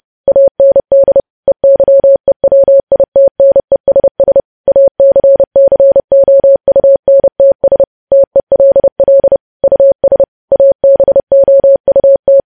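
Morse code, heard as a keyed tone, about 30 wpm, 550 Hz. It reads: AND EYEWITNESS ACCOUNTS TELL US ABOUT
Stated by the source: Morse code news headlines sent at 30 words per minute.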